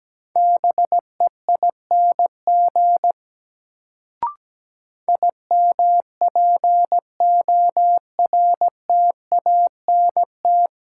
Morse code, heard as a keyed tone, about 17 wpm, 700 Hz. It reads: BEING IMPORTANT
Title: Sentences from Top 600 Words - New Word Drill - 17wpm